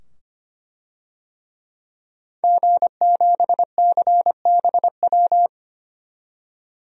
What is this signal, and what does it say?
Morse code, heard as a keyed tone, about 25 wpm, 700 Hz. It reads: G7CBW
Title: DXCC-100 Call Signs - 5 Characters - 25wpm